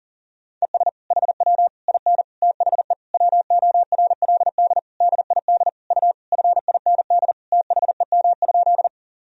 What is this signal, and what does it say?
Morse code, heard as a keyed tone, about 40 wpm, 700 Hz. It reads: ES HW IN THE WORLD DID U FIND THEM?
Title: Sentences from Top 200 Words and Abbreviations 40wpm